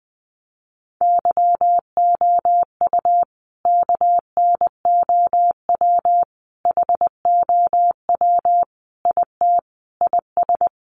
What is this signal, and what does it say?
Morse code, heard as a keyed tone, about 20 wpm, 700 Hz. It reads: YOU KNOW HOW IT IS
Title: Sentences from Top 100 Words 20wpm